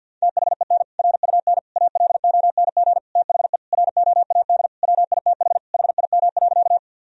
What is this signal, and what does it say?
Morse code, heard as a keyed tone, about 49 wpm, 700 Hz. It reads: THEN RUN ALONG THE ROAD WITH HIM.
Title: Sentences from Top 1000 Words - Review Part 2 - 50wpm